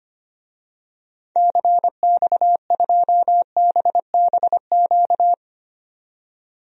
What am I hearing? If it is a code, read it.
Morse code, heard as a keyed tone, about 25 wpm, 700 Hz. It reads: CX2BBQ
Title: DXCC-100 Call Signs - 6 Characters - 25wpm